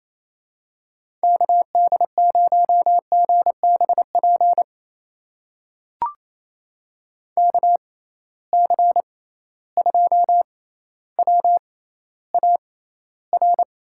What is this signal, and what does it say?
Morse code, heard as a keyed tone, about 28 wpm, spaced as for 14 wpm, 700 Hz. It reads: KD0GBP KC2WAR
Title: USA Call Signs - 6 Characters - 6x Character Spacing - 28wpm